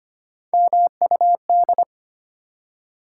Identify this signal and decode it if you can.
Morse code, heard as a keyed tone, about 25 wpm, 700 Hz. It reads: MUD
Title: Three Letter Words - 25wpm